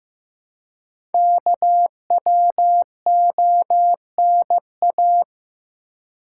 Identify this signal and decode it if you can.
Morse code, heard as a keyed tone, about 15 wpm, 700 Hz. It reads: KWONA
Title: Code Groups - 5 Letters - 15wpm